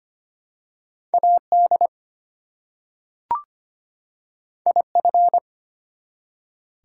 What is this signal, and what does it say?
Morse code, heard as a keyed tone, about 25 wpm, 700 Hz. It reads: AD IF
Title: Two Letter Words - 25wpm